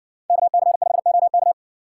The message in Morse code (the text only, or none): KC5OK